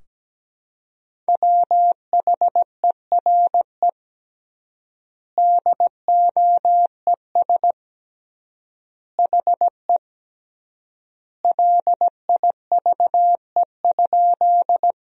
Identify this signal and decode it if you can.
Morse code, heard as a keyed tone, about 17 wpm, 700 Hz. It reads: WHERE DOES HE LIVE?
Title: 3x Word-Spacing - Sentences from Top 200 Words - 17wpm